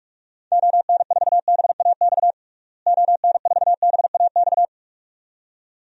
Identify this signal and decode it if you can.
Morse code, heard as a keyed tone, about 46 wpm, 700 Hz. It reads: ON4BAX ON4BAX
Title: DXCC-100 Call Signs - 6 Characters - Repeated 2x - 45wpm